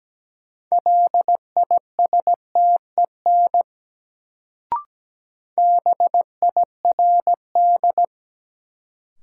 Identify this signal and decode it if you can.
Morse code, heard as a keyed tone, about 17 wpm, 700 Hz. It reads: LISTEN BIRD
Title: Sentences from Top 400 Words - New Word Drill - 17wpm